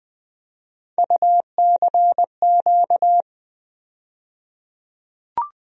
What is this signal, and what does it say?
Morse code, heard as a keyed tone, about 20 wpm, 700 Hz. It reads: UCQ
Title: USA Call Sign Postfixes - 3 Letters 20wpm